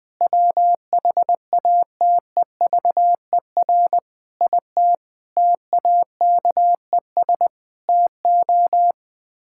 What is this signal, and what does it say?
Morse code, heard as a keyed tone, about 20 wpm, 700 Hz. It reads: WHATEVER IT TAKES TO
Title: Sets of 6 Words - 20wpm